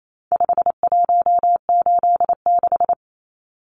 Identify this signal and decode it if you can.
Morse code, heard as a keyed tone, about 28 wpm, 700 Hz. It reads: 5186
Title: Numbers - 4-Digit 28wpm